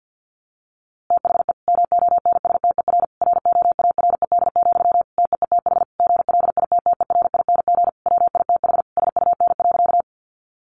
Morse code, heard as a keyed tone, about 49 wpm, 700 Hz. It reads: THE MONSTER ROARED, TEETH GLITTERING WITH SUN.
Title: A Sound Of Thunder 50wpm